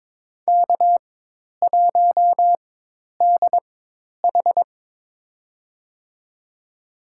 Morse code, heard as a keyed tone, about 21 wpm, spaced as for 10 wpm, 700 Hz. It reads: K1DH
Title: USA Call Signs - 4 Characters - 4x Character Spacing - 22wpm